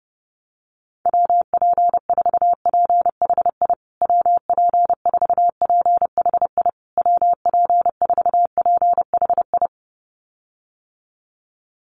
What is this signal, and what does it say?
Morse code, heard as a keyed tone, about 30 wpm, 700 Hz. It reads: WP4PHI WP4PHI WP4PHI
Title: USA Call Signs Repeated 3x 30wpm